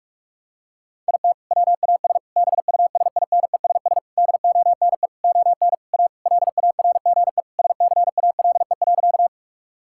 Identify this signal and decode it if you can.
Morse code, heard as a keyed tone, about 46 wpm, 700 Hz. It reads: IT WAS BUSINESS DONE ON A LARGE SCALE.
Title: Sentences from Top 1000 Words - Review Part 5 - 45wpm